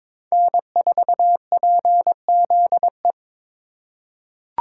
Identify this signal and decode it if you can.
Morse code, heard as a keyed tone, about 23 wpm, 700 Hz. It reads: N4PZE
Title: USA Call Signs - 5 Characters - 2x Character Spacing - 22wpm